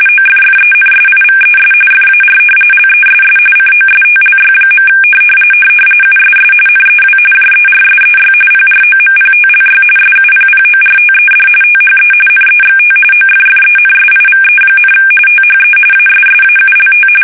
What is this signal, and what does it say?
RTTY, about 45.45 baud. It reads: OFFENSIVES FAR LESS COSTLY. USS NEW JERSEY PROMPTLY SUPPORTED THE NEXT OFFENSIVE, THE ASSAULT
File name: fsk850 / battleshipnj